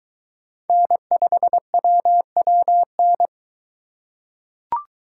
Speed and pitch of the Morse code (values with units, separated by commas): 21 wpm, 700 Hz